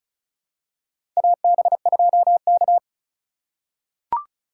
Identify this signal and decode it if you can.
Morse code, heard as a keyed tone, about 36 wpm, 690 Hz. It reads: AB2K
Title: USA Call Signs - 4 Characters - 3x Character Spacing - 35wpm